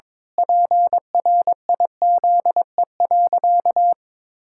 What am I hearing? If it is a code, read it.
Morse code, heard as a keyed tone, about 22 wpm, 700 Hz. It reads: PRIZE.
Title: Pangram Challenge v2 - 22wpm